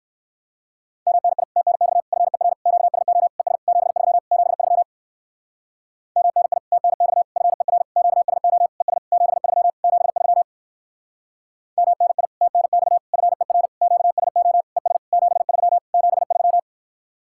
Morse code, heard as a keyed tone, about 60 wpm, 700 Hz. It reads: MNI TNX FER QSO ES 73 73 MNI TNX FER QSO ES 73 73 MNI TNX FER QSO ES 73 73